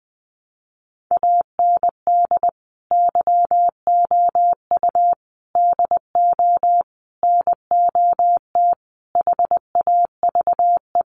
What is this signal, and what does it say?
Morse code, heard as a keyed tone, about 20 wpm, 700 Hz. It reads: AND YOU DO NOT HAVE
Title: Sets of 5 Words Encore 20wpm